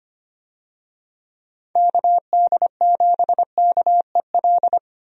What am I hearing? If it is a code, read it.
Morse code, heard as a keyed tone, about 25 wpm, 700 Hz. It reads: KD7KEL
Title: USA Call Signs - 6 Characters - 25wpm